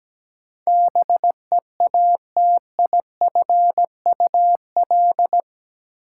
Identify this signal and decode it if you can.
Morse code, heard as a keyed tone, about 17 wpm, 700 Hz. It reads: BEATIFUL
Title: Sentences from Top 1300 Words - New Word Drill - 17wpm